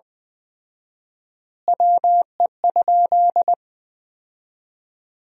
Morse code, heard as a keyed tone, about 20 wpm, 700 Hz. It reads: WE?
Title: 4x Word-Spacing - Sentences from Top 200 Words - 20wpm